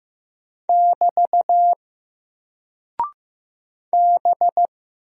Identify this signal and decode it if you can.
Morse code, heard as a keyed tone, about 15 wpm, 700 Hz. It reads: <BT> B
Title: Single Character (Letter:Number:Punctuation:Prosign) Alt 15wpm